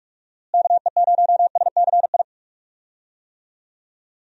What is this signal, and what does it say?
Morse code, heard as a keyed tone, about 46 wpm, 700 Hz. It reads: KE0SCI
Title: USA Call Signs - 6 Characters - Repeated 2x - 45wpm